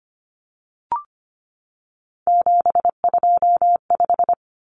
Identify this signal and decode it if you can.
Morse code, heard as a keyed tone, about 25 wpm, 700 Hz. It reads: 725
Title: Common Cut Numbers - 3-Digit 25wpm